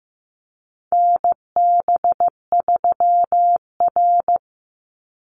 Morse code, heard as a keyed tone, about 15 wpm, 700 Hz. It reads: NB3R
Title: Call Signs Repeated 3x - 15wpm